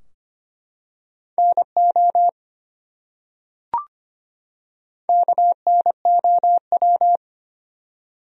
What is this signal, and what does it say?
Morse code, heard as a keyed tone, about 25 wpm, 700 Hz. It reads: NO KNOW
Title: Top 100 Words 25wpm